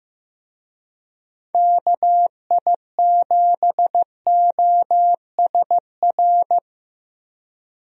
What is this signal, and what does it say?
Morse code, heard as a keyed tone, about 15 wpm, 700 Hz. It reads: KI7OSR
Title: USA Call Signs - 6 Characters - 15wpm